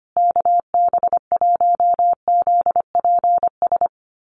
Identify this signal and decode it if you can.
Morse code, heard as a keyed tone, about 25 wpm, 700 Hz. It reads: KB1ZPS
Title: USA Call Signs Repeated 2x 25wpm